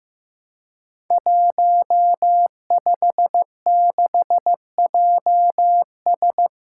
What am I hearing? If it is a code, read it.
Morse code, heard as a keyed tone, about 15 wpm, 700 Hz. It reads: 156JS
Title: Five Challenging Letter-Number Combinations 15wpm